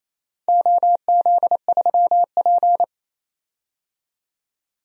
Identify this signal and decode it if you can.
Morse code, heard as a keyed tone, about 28 wpm, 700 Hz. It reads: OZ3P
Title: DXCC-100 Call Signs - 4 Characters - 28wpm